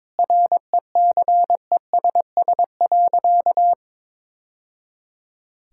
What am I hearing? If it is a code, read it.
Morse code, heard as a keyed tone, about 22 wpm, 700 Hz. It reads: RECESS.
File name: practice / 22wpm / Easy Sentences with Hints - 22wpm